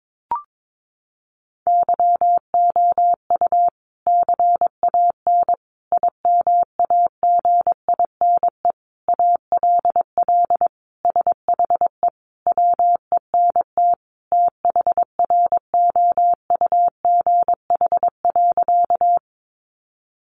Morse code, heard as a keyed tone, about 22 wpm, 700 Hz. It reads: YOU CAN IMAGINE ALL SHE WENT THROUGH.
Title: Sentences from Top 1400 Words 22wpm